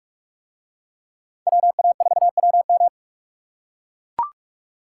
Morse code, heard as a keyed tone, about 41 wpm, 700 Hz. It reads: WA4WM